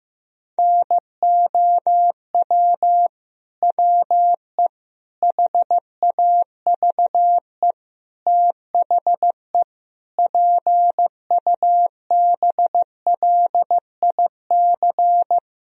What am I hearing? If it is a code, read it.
Morse code, heard as a keyed tone, about 16 wpm, 700 Hz. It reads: NOW WE HAVE THE PUBLIC
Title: Sentences from Top 1000 Words - Review Part 3 - 15wpm